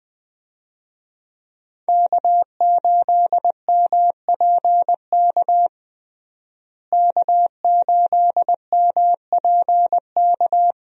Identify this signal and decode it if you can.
Morse code, heard as a keyed tone, about 20 wpm, 700 Hz. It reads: K8MPK K8MPK
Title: USA Call Signs - 5 Characters - Repeated 2x - 20wpm